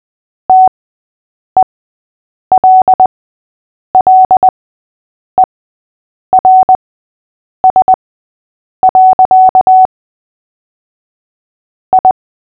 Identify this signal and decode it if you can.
Morse code, heard as a keyed tone, about 20 wpm, spaced as for 8 wpm, 750 Hz. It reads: TELLERS. I
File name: cw / fists / Short-Story-8-8wpm